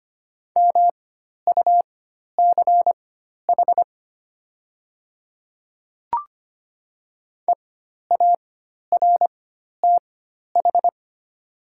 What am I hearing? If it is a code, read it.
Morse code, heard as a keyed tone, about 25 wpm, spaced as for 12 wpm, 700 Hz. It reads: MUCH EARTH
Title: 4x Character Spacing - Top 200 Words - 25wpm